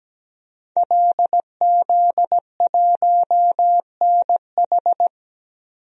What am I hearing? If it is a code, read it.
Morse code, heard as a keyed tone, about 17 wpm, 700 Hz. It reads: LZ1NH
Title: DXCC-100 Call Signs - 5 Characters - Repeated 2x - 17wpm